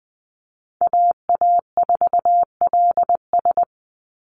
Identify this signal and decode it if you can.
Morse code, heard as a keyed tone, about 20 wpm, 700 Hz. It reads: AA4LS
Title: Call Signs Repeated 2x - 20wpm